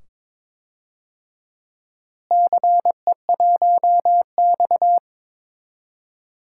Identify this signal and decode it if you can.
Morse code, heard as a keyed tone, about 21 wpm, 700 Hz. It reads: CE1X
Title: DXCC-100 Call Signs - 4 Characters - Repeated 2x - 22wpm